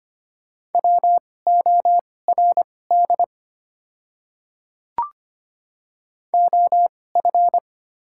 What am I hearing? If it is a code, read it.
Morse code, heard as a keyed tone, about 25 wpm, spaced as for 18 wpm, 700 Hz. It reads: WORD OF
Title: 2x Character Spacing - Top 100 Words - 25wpm